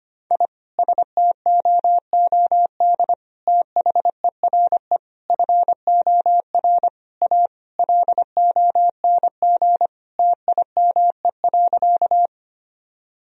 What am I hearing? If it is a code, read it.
Morse code, heard as a keyed tone, about 25 wpm, 700 Hz. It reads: I STOOD THERE FOR A LONG TIME.
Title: Sentences from Top 1000 Words - Review Part 2 - 25wpm